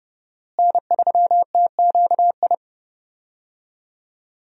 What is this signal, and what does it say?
Morse code, heard as a keyed tone, about 31 wpm, 700 Hz. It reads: N3TQI
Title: USA Call Signs - 5 Characters - Repeated 2x - 30wpm